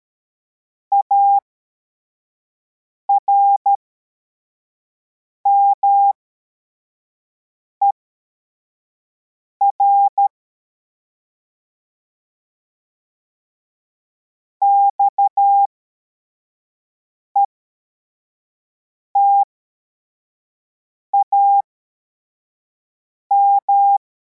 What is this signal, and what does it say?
Morse code, heard as a keyed tone, about 13 wpm, spaced as for 4 wpm, 800 Hz. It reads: ARMER XETAM
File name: formation / cw / Text24